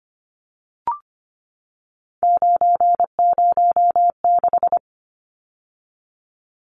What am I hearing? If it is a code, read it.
Morse code, heard as a keyed tone, about 25 wpm, 700 Hz. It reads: 906
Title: Numbers - 3-Digit 25wpm